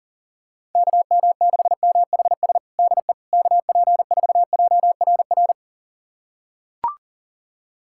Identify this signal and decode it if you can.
Morse code, heard as a keyed tone, about 40 wpm, 700 Hz. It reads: KM6MHS DE KP4JRR
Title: USA Callsign DE Callsign - 6 Characters - 40wpm